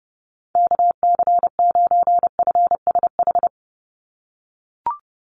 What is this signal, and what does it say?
Morse code, heard as a keyed tone, about 30 wpm, 700 Hz. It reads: KC9FSH
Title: USA Call Signs 30wpm